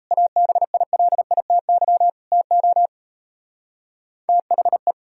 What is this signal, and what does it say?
Morse code, heard as a keyed tone, about 34 wpm, 700 Hz. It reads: ABILITY TO THE
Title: Speed-Racing - Sets of 3 Words 28wpm